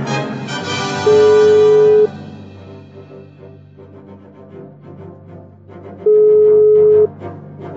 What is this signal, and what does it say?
Ringback tone (Europe, 425 Hz).